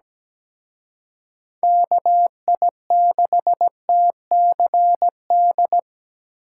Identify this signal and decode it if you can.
Morse code, heard as a keyed tone, about 20 wpm, 700 Hz. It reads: KI6TCD
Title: Speed Racing - USA Call Signs 17wpm